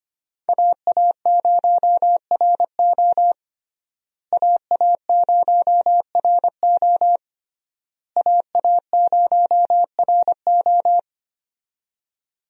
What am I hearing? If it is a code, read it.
Morse code, heard as a keyed tone, about 25 wpm, 700 Hz. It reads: AA0RO AA0RO AA0RO
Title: USA Call Signs - 5 Characters - Repeated 3x - 25wpm